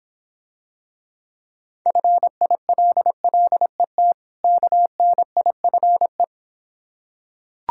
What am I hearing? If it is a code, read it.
Morse code, heard as a keyed tone, about 25 wpm, 700 Hz. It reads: FILLET KNIFE